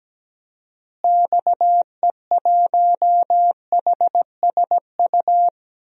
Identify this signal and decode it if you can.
Morse code, heard as a keyed tone, about 17 wpm, 700 Hz. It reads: XE1HSU
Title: DXCC-100 Call Signs - 6 Characters - Repeated 3x - 17wpm